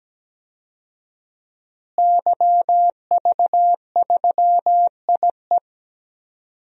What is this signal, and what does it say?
Morse code, heard as a keyed tone, about 17 wpm, 700 Hz. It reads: YV3IE
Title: DXCC-100 Call Signs - 5 Characters - Repeated 2x - 17wpm